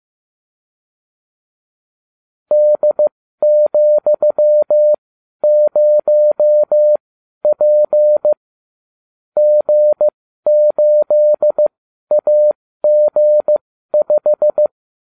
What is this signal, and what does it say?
Morse code, heard as a keyed tone, about 15 wpm, spaced as for 12 wpm, 600 Hz. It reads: D,0P G8AG5